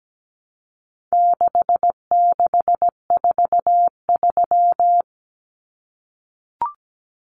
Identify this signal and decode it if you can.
Morse code, heard as a keyed tone, about 17 wpm, 700 Hz. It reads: 6643
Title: Common Cut Numbers - 4-Digit 17wpm